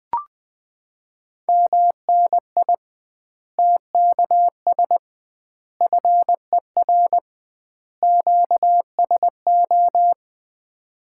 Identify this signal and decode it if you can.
Morse code, heard as a keyed tone, about 20 wpm, 700 Hz. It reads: MNI TKS FER QSO
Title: Common QSO Phrases - 2x Word-Spacing - 20wpm